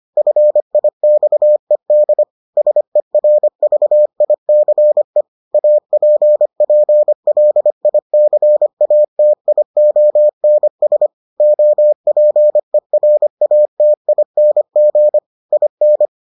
Morse code, heard as a keyed tone, about 25 wpm, 600 Hz. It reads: FIXED SERVICE APPLICATIONS OPERATING IN